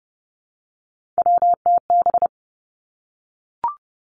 Morse code, heard as a keyed tone, about 31 wpm, 700 Hz. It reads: WTB
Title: Three Challenging Letter-Number Combinations 30wpm